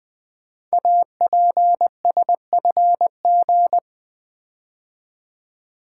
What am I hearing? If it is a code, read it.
Morse code, heard as a keyed tone, about 20 wpm, 700 Hz. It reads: APSFG
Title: Code Groups - 5 Letters - 20wpm